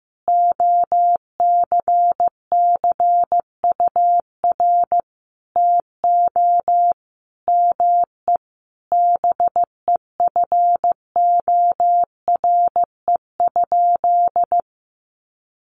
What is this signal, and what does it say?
Morse code, heard as a keyed tone, about 14 wpm, 700 Hz. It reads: OCCUR TO ME BEFORE?
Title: Sentences from Top 900 Words 15wpm